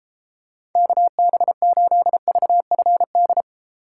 Morse code, heard as a keyed tone, about 32 wpm, 700 Hz. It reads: KB8VFD